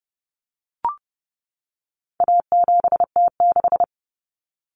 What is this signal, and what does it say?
Morse code, heard as a keyed tone, about 30 wpm, 700 Hz. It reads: A7T6
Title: Common Cut Numbers - 4-Digit 30wpm